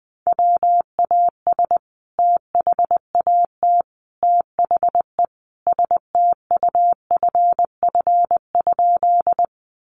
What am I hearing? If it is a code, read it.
Morse code, heard as a keyed tone, about 20 wpm, 700 Hz. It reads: WAS THAT THE STUFF?
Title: Sentences from Top 1400 Words 20wpm